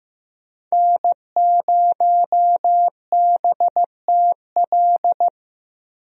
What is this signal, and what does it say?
Morse code, heard as a keyed tone, about 15 wpm, 700 Hz. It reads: N0BTL
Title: USA Call Signs - 5 Characters - Repeated 3x - 15wpm